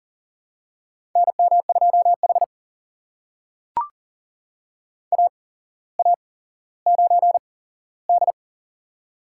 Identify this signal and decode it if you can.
Morse code, heard as a keyed tone, about 40 wpm, spaced as for 17 wpm, 700 Hz. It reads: NM2H AA9D
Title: USA Call Signs - 4 Characters - 8x Character Spacing - 40wpm